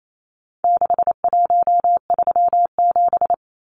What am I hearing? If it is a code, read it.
Morse code, heard as a keyed tone, about 28 wpm, 700 Hz. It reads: 6137
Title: Numbers - 4-Digit 28wpm